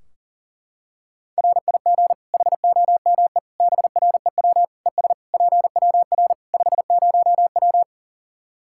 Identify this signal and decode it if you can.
Morse code, heard as a keyed tone, about 40 wpm, 700 Hz. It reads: RIG HOME BREW ES PWR 50W
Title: K1USN SST QSO Practice 40wpm